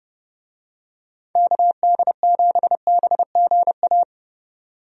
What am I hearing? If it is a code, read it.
Morse code, heard as a keyed tone, about 30 wpm, 700 Hz. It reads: KD7BGA
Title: USA Call Signs - 6 Characters - 30wpm